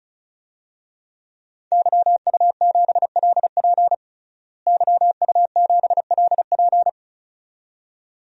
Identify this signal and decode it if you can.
Morse code, heard as a keyed tone, about 36 wpm, 700 Hz. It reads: YU7LP YU7LP
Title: DXCC-100 Call Signs - 5 Characters - Repeated 2x - 35wpm